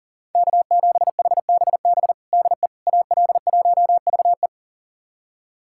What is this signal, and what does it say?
Morse code, heard as a keyed tone, about 40 wpm, 700 Hz. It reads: K7HBB DE AL1VE
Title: USA Callsign DE Callsign - 5 Characters - 40wpm